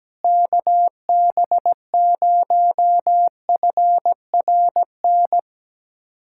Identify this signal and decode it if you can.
Morse code, heard as a keyed tone, about 17 wpm, 700 Hz. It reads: KB0FRN
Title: USA Callsign DE Callsign - 6 Characters - 17wpm